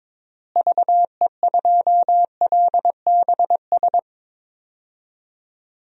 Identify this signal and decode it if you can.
Morse code, heard as a keyed tone, about 22 wpm, 700 Hz. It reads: VE2LBS
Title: DXCC-100 Call Signs - 6 Characters - Repeated 2x - 22wpm